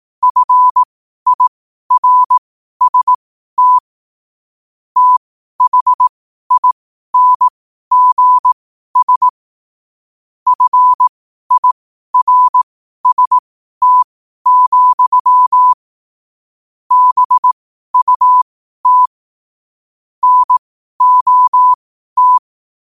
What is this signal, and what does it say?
Morse code, heard as a keyed tone, about 18 wpm, spaced as for 12 wpm, 1000 Hz. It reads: FIRST THINGS FIRST, BUT NOT